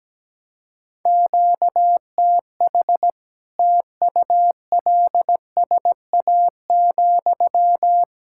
Morse code, heard as a keyed tone, about 17 wpm, 700 Hz. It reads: QTH TULSA,
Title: Practice QSO - Exchange 1 - Beginning - 17wpm